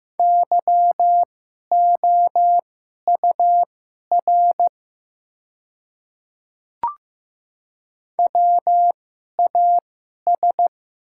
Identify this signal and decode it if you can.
Morse code, heard as a keyed tone, about 15 wpm, spaced as for 11 wpm, 700 Hz. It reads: YOUR WAS
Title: 2x Character Spacing - Top 100 Words - 15wpm